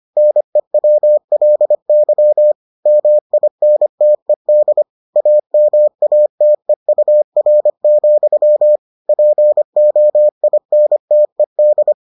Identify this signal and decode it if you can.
Morse code, heard as a keyed tone, about 25 wpm, 600 Hz. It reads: NEWLY MINTED AMATEUR, POINTED